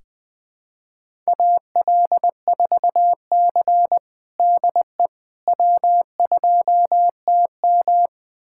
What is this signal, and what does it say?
Morse code, heard as a keyed tone, about 20 wpm, 700 Hz. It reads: AL4C DE W2TM
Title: USA Callsign DE Callsign - 4 Characters - 20wpm